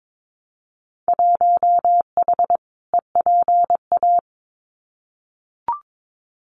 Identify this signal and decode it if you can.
Morse code, heard as a keyed tone, about 22 wpm, 700 Hz. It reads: 1H EPA
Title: Winter Field Day Exchanges 22wpm